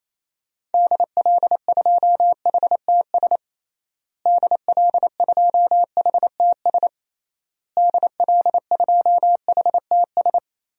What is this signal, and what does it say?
Morse code, heard as a keyed tone, about 28 wpm, 700 Hz. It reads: DL2HTS DL2HTS DL2HTS
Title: DXCC-100 Call Signs - 6 Characters - Repeated 3x - 28wpm